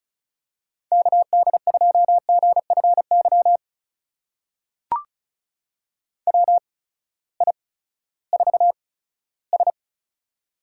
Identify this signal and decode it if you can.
Morse code, heard as a keyed tone, about 36 wpm, spaced as for 18 wpm, 700 Hz. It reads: KD2GFY WI4S
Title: USA Call Signs - 8x Character Spacing - 35wpm